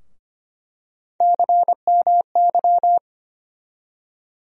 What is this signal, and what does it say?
Morse code, heard as a keyed tone, about 25 wpm, 700 Hz. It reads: CMY
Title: USA Call Sign Postfixes - 3 Letters 25wpm